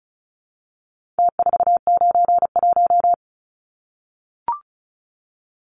Morse code, heard as a keyed tone, about 34 wpm, 700 Hz. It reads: T491
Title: Numbers - Contest Style - 4-Digit 35wpm